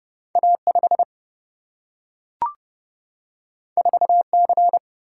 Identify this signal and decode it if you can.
Morse code, heard as a keyed tone, about 30 wpm, 700 Hz. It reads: A5 4C
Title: Single-Letter Single-Number 30wpm